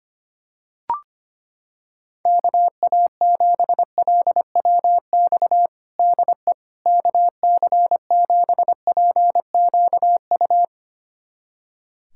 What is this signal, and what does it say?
Morse code, heard as a keyed tone, about 25 wpm, 700 Hz. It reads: KA7LWX DE KC7PQU
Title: USA Callsign DE Callsign - 6 Characters - 25wpm